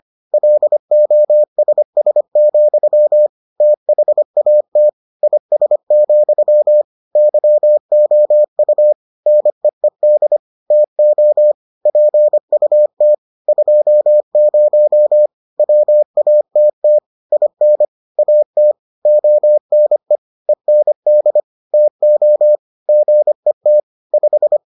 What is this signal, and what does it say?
Morse code, heard as a keyed tone, about 25 wpm, 600 Hz. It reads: LOSS, THAT IS, YOU NEED TO PUT 20 WATT IN AT ONE END TO GET 5